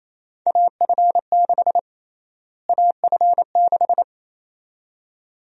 Morse code, heard as a keyed tone, about 28 wpm, 700 Hz. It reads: AF6 AF6
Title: USA Call Sign Prefixes - Repeated 2x - 28wpm